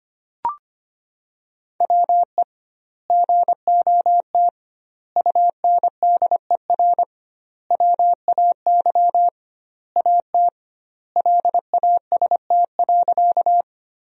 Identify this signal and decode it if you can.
Morse code, heard as a keyed tone, about 25 wpm, 700 Hz. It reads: WE GOT UNDER WAY AT LAST.
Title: 2x Word-Spacing - Sentences from Top 300 Words - 25wpm